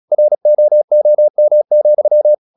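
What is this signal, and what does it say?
Morse code, heard as a keyed tone, about 37 wpm, 600 Hz. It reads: ROOM,
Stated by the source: Morse practice at 36 wpm, text of Ken Follett's novel Never, from CW med Gnister.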